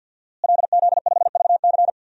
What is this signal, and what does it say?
Morse code, heard as a keyed tone, about 46 wpm, 700 Hz. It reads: LZ5VC
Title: DXCC-100 Call Signs - 5 Characters - Repeated 3x - 50wpm